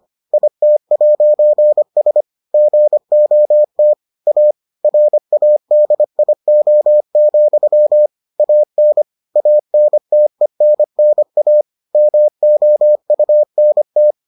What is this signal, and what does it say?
Morse code, heard as a keyed tone, about 25 wpm, 600 Hz. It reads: IT'S GOT A RADIO, AN ANTENNA MOUNT